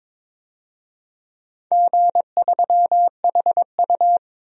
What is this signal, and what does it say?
Morse code, heard as a keyed tone, about 22 wpm, 700 Hz. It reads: G3HU
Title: DXCC-100 Call Signs - 4 Characters - 22wpm